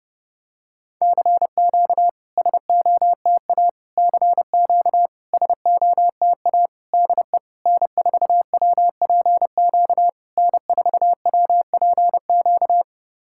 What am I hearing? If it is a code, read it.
Morse code, heard as a keyed tone, about 29 wpm, 690 Hz. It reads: CQ SOTA CQ SOTA DE N4WPQ N4WPQ
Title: SOTA QSO Practice 30wpm